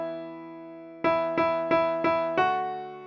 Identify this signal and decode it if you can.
There is no code.